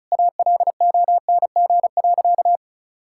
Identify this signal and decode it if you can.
Morse code, heard as a keyed tone, about 36 wpm, 700 Hz. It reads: ALONG.
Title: Sentences from Top 1600 Words 35wpm